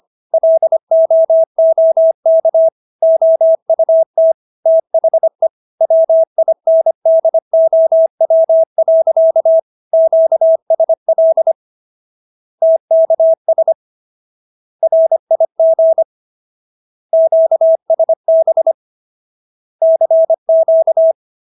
Morse code, heard as a keyed tone, about 25 wpm, 650 Hz. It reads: LOOK OUT THE WINDOW. QSL TKS RIG QSB CQ